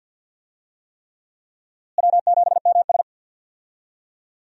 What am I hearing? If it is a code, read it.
Morse code, heard as a keyed tone, about 46 wpm, 700 Hz. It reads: W7MS
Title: USA Call Signs - 4 Characters - Repeated 3x - 50wpm